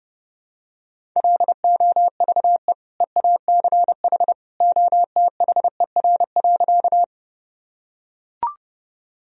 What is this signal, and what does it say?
Morse code, heard as a keyed tone, about 32 wpm, 700 Hz. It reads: LOVE EACH OTHER.
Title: Sentences from Top 1000 Words - Review Part 1 - 30wpm